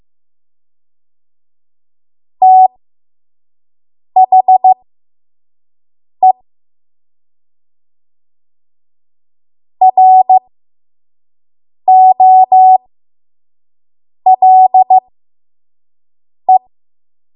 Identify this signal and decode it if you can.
Morse code, heard as a keyed tone, about 15 wpm, spaced as for 5 wpm, 750 Hz. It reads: THE ROLE
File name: Morse / 05WPM